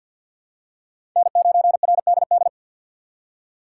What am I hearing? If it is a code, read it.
Morse code, heard as a keyed tone, about 53 wpm, 700 Hz. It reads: N9RDD
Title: USA Call Signs - 5 Characters - 4x Character Spacing - 50wpm